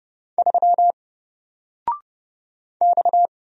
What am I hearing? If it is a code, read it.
Morse code, heard as a keyed tone, about 30 wpm, 700 Hz. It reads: 3 X
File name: Single-Letter-Number-30wpm